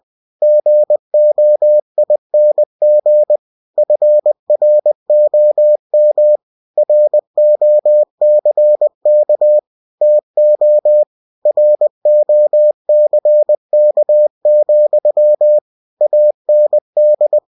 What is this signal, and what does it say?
Morse code, heard as a keyed tone, about 20 wpm, 600 Hz. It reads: GOING FROM ROCK TO ROCK, AND
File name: winnie-20wpm